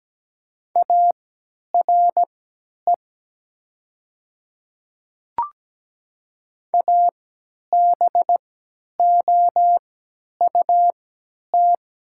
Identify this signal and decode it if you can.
Morse code, heard as a keyed tone, about 17 wpm, spaced as for 10 wpm, 700 Hz. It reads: ARE ABOUT
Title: 3x Character Spacing - Top 200 Words - 17wpm